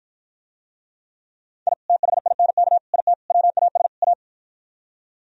Morse code, heard as a keyed tone, about 52 wpm, 700 Hz. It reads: I THINK IT WAS A